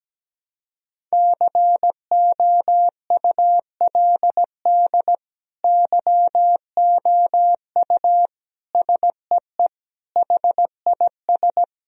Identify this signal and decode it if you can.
Morse code, heard as a keyed tone, about 17 wpm, 700 Hz. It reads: COULD YOU SEE HIS
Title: Sentences from Top 1000 Words - Review Part 2 - 17wpm